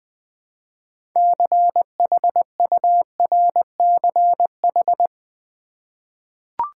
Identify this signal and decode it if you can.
Morse code, heard as a keyed tone, about 21 wpm, 700 Hz. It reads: CHURCH
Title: Six Letter Words - 20wpm